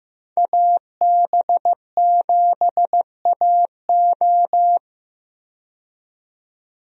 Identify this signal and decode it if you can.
Morse code, heard as a keyed tone, about 15 wpm, 700 Hz. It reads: AB7AO
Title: USA Call Signs - 5 Characters - 15wpm